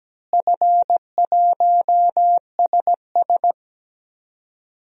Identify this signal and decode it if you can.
Morse code, heard as a keyed tone, about 17 wpm, 700 Hz. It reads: F1SS
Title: DXCC-100 Call Signs - 4 Characters - Repeated 2x - 17wpm